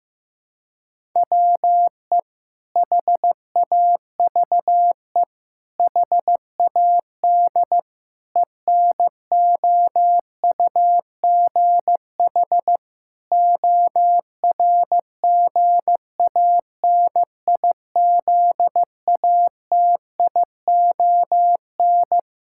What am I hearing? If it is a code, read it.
Morse code, heard as a keyed tone, about 15 wpm, 700 Hz. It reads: WE HAVE HAD ENOUGH ORGANIZATION
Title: Sentences from Top 1000 Words - Review Part 5 - 15wpm